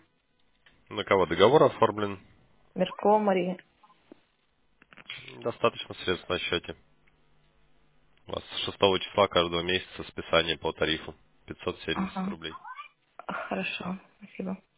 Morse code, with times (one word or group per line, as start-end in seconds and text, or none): none